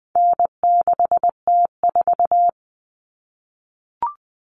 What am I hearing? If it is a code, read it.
Morse code, heard as a keyed tone, about 20 wpm, 700 Hz. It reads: N6T4
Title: All Cut Numbers - 4-Digit 20wpm